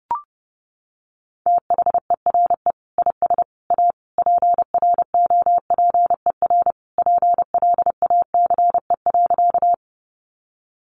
Morse code, heard as a keyed tone, about 30 wpm, 700 Hz. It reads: THERE IS A PROPER PLACE.
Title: Sentences from Top 800 Words 30wpm